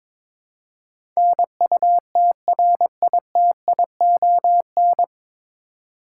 Morse code, heard as a keyed tone, about 21 wpm, 700 Hz. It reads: NUTRITION
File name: 9 Letter Words - First 3 Letters Given - 22wpm